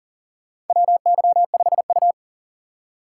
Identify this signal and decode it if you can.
Morse code, heard as a keyed tone, about 40 wpm, 700 Hz. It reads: WY5U
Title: USA Call Signs - 4 Characters - 3x Character Spacing - 40wpm